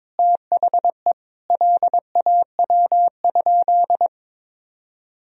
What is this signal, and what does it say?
Morse code, heard as a keyed tone, about 22 wpm, 700 Hz. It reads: THE LAW?
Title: Sentences from Top 1000 Words - Review Part 4 - 22wpm